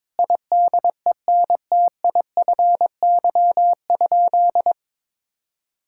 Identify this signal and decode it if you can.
Morse code, heard as a keyed tone, about 22 wpm, 700 Hz. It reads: IDENTIFY?
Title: Sentences from Top 1000 Words - Review Part 6 - 22wpm